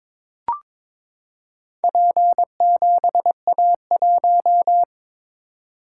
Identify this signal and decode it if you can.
Morse code, heard as a keyed tone, about 22 wpm, 700 Hz. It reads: P7A1
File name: practice / 22wpm / Four Letter-Number Combinations - 22wpm